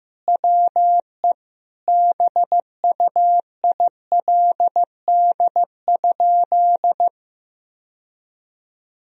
Morse code, heard as a keyed tone, about 15 wpm, 700 Hz. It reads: WE BUILD?
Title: Sentences from Top 1000 Words - Review Part 3 - 15wpm